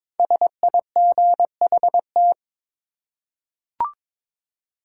Morse code, heard as a keyed tone, about 22 wpm, 700 Hz. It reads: SIGHT
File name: 5 Letter Words - First 3 Letters Given - 22wpm